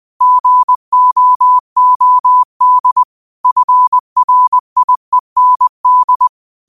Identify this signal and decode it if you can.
Morse code, heard as a keyed tone, about 20 wpm, 1000 Hz. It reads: GOOD FRIEND